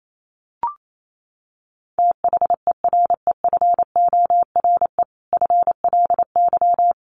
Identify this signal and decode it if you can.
Morse code, heard as a keyed tone, about 28 wpm, 700 Hz. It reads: THEREFORE FLY
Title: Sentences from Top 800 Words 28wpm